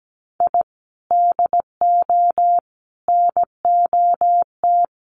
Morse code, heard as a keyed tone, about 17 wpm, 700 Hz. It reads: I DO NOT
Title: Sets of 5 Words 17wpm